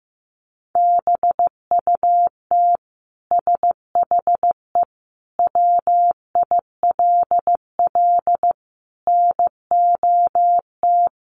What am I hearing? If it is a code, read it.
Morse code, heard as a keyed tone, about 15 wpm, 700 Hz. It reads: BUT SHE WILL NOT
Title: Sentences from Top 1200 Words 15wpm